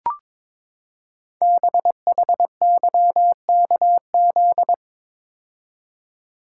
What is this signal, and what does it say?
Morse code, heard as a keyed tone, about 22 wpm, 700 Hz. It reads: BHYKZ
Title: Code Groups - 5 Letters - 22wpm